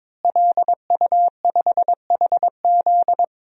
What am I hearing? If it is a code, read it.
Morse code, heard as a keyed tone, about 22 wpm, 700 Hz. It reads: LU5HZ
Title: DXCC-100 Call Signs - 5 Characters - Repeated 2x - 22wpm